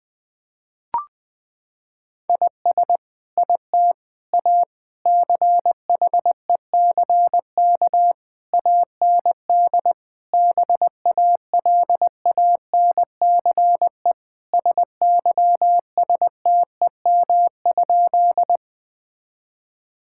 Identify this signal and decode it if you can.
Morse code, heard as a keyed tone, about 20 wpm, 700 Hz. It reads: IS IT A CHECK AND BALANCE SYSTEM?
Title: Sentences from Top 1000 Words - Review Part 6 - 20wpm